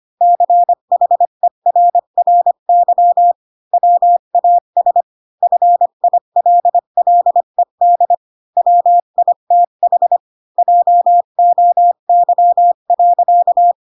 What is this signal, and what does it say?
Morse code, heard as a keyed tone, about 25 wpm, 700 Hz. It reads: CHERRY WAS FILLED WITH JOY.
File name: Pinocchio / Pinocchio ch1 20 25 ik1hge